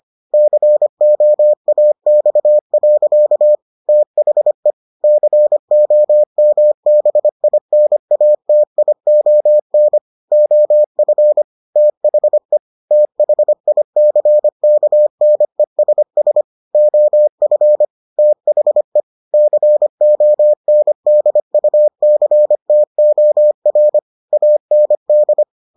Morse code, heard as a keyed tone, about 25 wpm, 600 Hz. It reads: COAX. THE COMBINATION OF THE THICKNESS OF THE CONDUCTOR AND